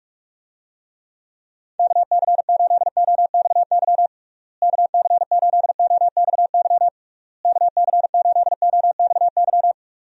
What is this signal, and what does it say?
Morse code, heard as a keyed tone, about 41 wpm, 700 Hz. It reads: KC8OXY KC8OXY KC8OXY
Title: USA Call Signs - 6 Characters - Repeated 3x - 45wpm